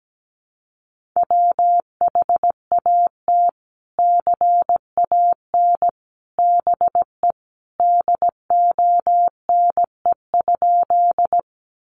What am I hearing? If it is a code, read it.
Morse code, heard as a keyed tone, about 17 wpm, 700 Hz. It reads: WHAT CAN BE DONE?
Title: Sentences from Top 500 Words 17wpm